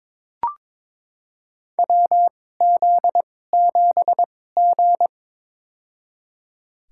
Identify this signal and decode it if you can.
Morse code, heard as a keyed tone, about 22 wpm, spaced as for 16 wpm, 700 Hz. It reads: WZ7G
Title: USA Call Signs - 4 Characters - 2x Character Spacing - 22wpm